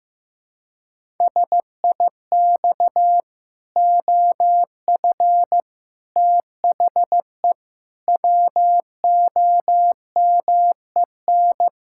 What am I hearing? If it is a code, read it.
Morse code, heard as a keyed tone, about 15 wpm, 700 Hz. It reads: SIX OF THE WOMEN
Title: Pangram Challenge v2 - 15wpm